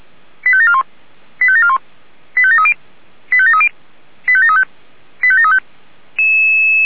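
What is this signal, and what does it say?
Five-tone selective call (ZVEI), heard as quick runs of five tones, tones of about 70 ms. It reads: 85741 85741 85730 85730 85735 85735
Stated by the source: ZVEI five-tone selective call received on SDR.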